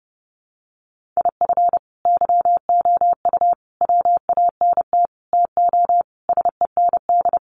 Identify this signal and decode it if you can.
Morse code, heard as a keyed tone, about 30 wpm, 700 Hz. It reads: IF YOU WANT TO SEND
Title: Sentences from Top 1500 Words 30wpm